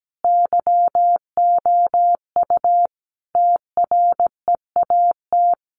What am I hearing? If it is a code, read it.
Morse code, heard as a keyed tone, about 17 wpm, 700 Hz. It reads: YOU TREAT
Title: Sentences from Top 1300 Words 17wpm